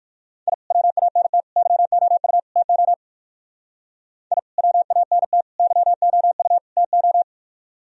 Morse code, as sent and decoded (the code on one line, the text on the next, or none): .. / .-- .- -. - / -.-- --- ..- / - --- / .. / .-- .- -. - / -.-- --- ..- / - ---
I WANT YOU TO I WANT YOU TO